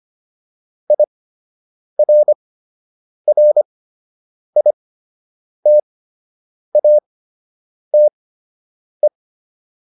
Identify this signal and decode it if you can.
Morse code, heard as a keyed tone, about 25 wpm, spaced as for 8 wpm, 600 Hz. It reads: IRRITATE